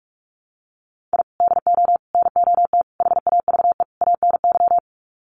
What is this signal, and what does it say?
Morse code, heard as a keyed tone, about 44 wpm, 700 Hz. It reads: I DO NOT HAVE ANY
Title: Sets of 5 Words 45wpm